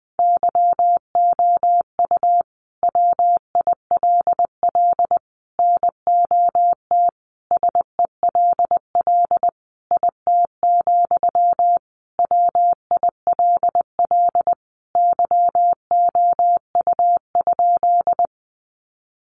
Morse code, heard as a keyed tone, about 20 wpm, 700 Hz. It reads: YOU WILL NOT SELL IT, WILL YOU?
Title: Sentences from Top 1100 Words 20wpm